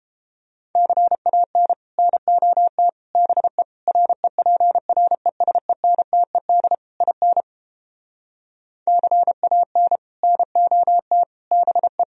Morse code, heard as a keyed tone, about 29 wpm, 700 Hz. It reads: CAN NOT BE REPRESENTED IN CAN NOT BE